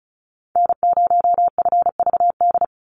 Morse code, heard as a keyed tone, about 34 wpm, 700 Hz. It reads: N0FVD
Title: USA Call Signs 35wpm